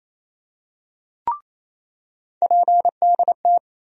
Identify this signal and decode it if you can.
Morse code, heard as a keyed tone, about 28 wpm, 700 Hz. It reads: PDT
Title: USA Call Sign Postfixes - 3 Letters 28wpm